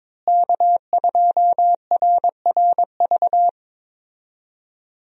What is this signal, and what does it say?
Morse code, heard as a keyed tone, about 21 wpm, 700 Hz. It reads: K2RRV
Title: USA Call Signs - 5 Characters - Repeated 3x - 22wpm